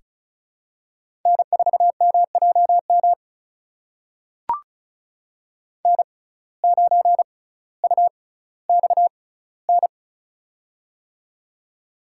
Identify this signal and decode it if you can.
Morse code, heard as a keyed tone, about 36 wpm, spaced as for 18 wpm, 700 Hz. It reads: N4MJM N9UXN
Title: USA Call Signs - 6x Character Spacing - 35wpm